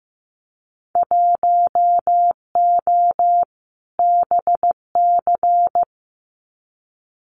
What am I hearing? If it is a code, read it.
Morse code, heard as a keyed tone, about 15 wpm, 700 Hz. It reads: 1O BC
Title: Winter Field Day Exchanges Encore 15wpm